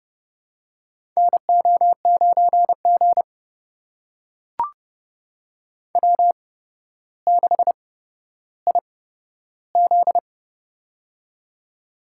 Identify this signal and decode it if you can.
Morse code, heard as a keyed tone, about 30 wpm, spaced as for 13 wpm, 700 Hz. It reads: NO9G W6IZ
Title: USA Call Signs - 8x Character Spacing - 30wpm